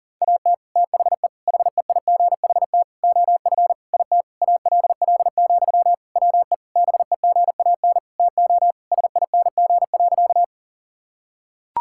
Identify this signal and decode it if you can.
Morse code, heard as a keyed tone, about 40 wpm, 700 Hz. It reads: AT THE HEIGHT OF IT ALL, WE BEGAN TO SING.
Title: Sentences from Top 1600 Words 40wpm